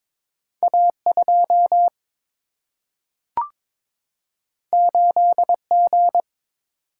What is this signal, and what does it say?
Morse code, heard as a keyed tone, about 22 wpm, 700 Hz. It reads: A 2 8 G
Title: Single-Letter Single-Number 22wpm